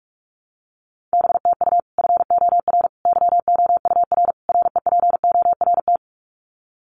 Morse code, heard as a keyed tone, about 44 wpm, 700 Hz. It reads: B TU FOR YOUR REPORT
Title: Common QSO Phrases 45wpm